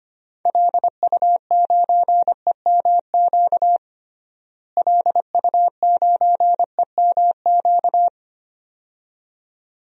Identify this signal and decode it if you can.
Morse code, heard as a keyed tone, about 25 wpm, 700 Hz. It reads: LU9EMQ LU9EMQ
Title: DXCC-100 Call Signs - 6 Characters - Repeated 2x - 25wpm